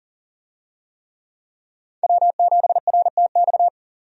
Morse code, heard as a keyed tone, about 40 wpm, 700 Hz. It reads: W7RTX
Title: USA Call Signs - 5 Characters - 40wpm